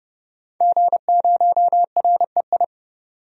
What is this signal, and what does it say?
Morse code, heard as a keyed tone, about 29 wpm, 690 Hz. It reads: G0REI
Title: DXCC-100 Call Signs - 5 Characters - 30wpm